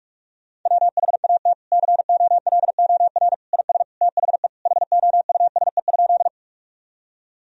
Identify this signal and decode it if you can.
Morse code, heard as a keyed tone, about 46 wpm, 700 Hz. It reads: WHAT COLOR IS THE HOUSE?
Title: Sentences from Top 1000 Words - Review Part 2 - 45wpm